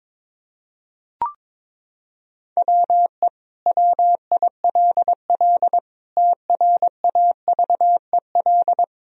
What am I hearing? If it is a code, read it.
Morse code, heard as a keyed tone, about 22 wpm, 700 Hz. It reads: WE WILL TRAVEL
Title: Sentences from Top 1000 Words - Review Part 2 - 22wpm